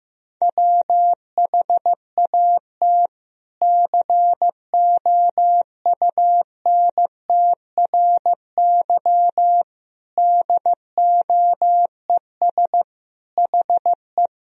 Morse code, heard as a keyed tone, about 15 wpm, 700 Hz. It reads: WHAT COUNTRY DOES HE
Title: Sentences from Top 300 Words and Abbreviations 15wpm